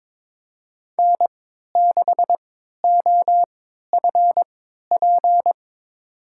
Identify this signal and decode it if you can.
Morse code, heard as a keyed tone, about 22 wpm, spaced as for 13 wpm, 700 Hz. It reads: N6OFP
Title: USA Call Signs - 5 Characters - 3x Character Spacing - 22wpm